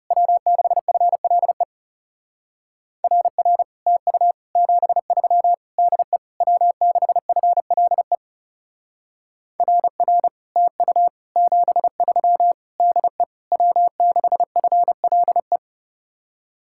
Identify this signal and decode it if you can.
Morse code, heard as a keyed tone, about 39 wpm, 700 Hz. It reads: W6FLE RR TU 73 DE W6FLE RR TU 73 DE W6FLE